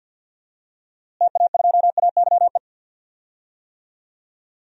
Morse code, heard as a keyed tone, about 53 wpm, 700 Hz. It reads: TA2AYE
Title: DXCC-100 Call Signs - 6 Characters - Repeated 2x - 50wpm